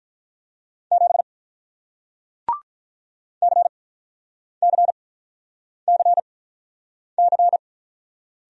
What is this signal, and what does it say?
Morse code, heard as a keyed tone, about 47 wpm, spaced as for 9 wpm, 700 Hz. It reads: 7 CCCC